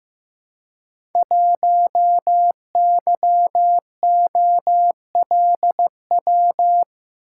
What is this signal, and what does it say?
Morse code, heard as a keyed tone, about 15 wpm, 700 Hz. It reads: 1YOLW
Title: Code Groups - 5 Letter:Numbers - 15wpm